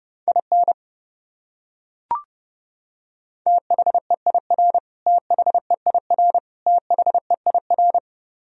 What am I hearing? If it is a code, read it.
Morse code, heard as a keyed tone, about 32 wpm, 700 Hz. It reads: IN THEIR THEIR THEIR
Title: Top 100 Words Repeated 30wpm